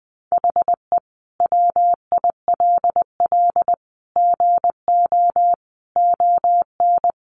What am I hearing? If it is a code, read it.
Morse code, heard as a keyed tone, about 20 wpm, 700 Hz. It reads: HE WILL GO ON
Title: Sentences from Top 1500 Words 20wpm